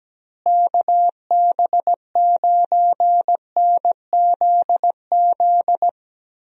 Morse code, heard as a keyed tone, about 17 wpm, 700 Hz. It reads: KB9NZZ
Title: USA Call Signs - 6 Characters - Repeated 2x - 17wpm